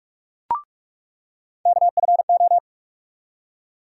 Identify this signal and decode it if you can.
Morse code, heard as a keyed tone, about 46 wpm, 700 Hz. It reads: KFO
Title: USA Call Sign Postfixes - 3 Letters 45wpm